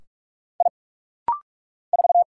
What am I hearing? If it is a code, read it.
Morse code, heard as a keyed tone, about 44 wpm, 700 Hz. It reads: I 4
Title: Single Letter-Number - Rapid-Fire 45wpm